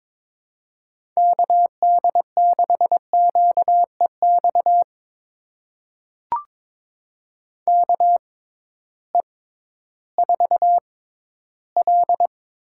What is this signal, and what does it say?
Morse code, heard as a keyed tone, about 22 wpm, spaced as for 13 wpm, 700 Hz. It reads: KD6QEX KE4L
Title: USA Call Signs - 6 Characters - 6x Character Spacing - 22wpm